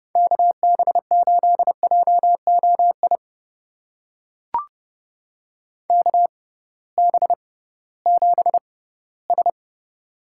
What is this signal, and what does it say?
Morse code, heard as a keyed tone, about 30 wpm, spaced as for 18 wpm, 700 Hz. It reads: KB8JOI KB7S
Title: USA Call Signs - 6x Character Spacing - 30wpm